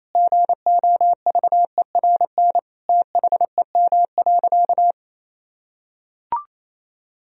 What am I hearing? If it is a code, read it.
Morse code, heard as a keyed tone, about 28 wpm, 700 Hz. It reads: GOVERN THEM.
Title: Sentences from Top 1000 Words - Review Part 2 - 28wpm